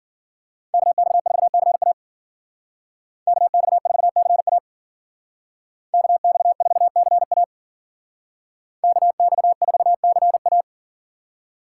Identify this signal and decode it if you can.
Morse code, heard as a keyed tone, about 49 wpm, 700 Hz. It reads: KX4CA KX4CA KX4CA KX4CA
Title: Speed Racing - USA 2x2 Call Signs 40wpm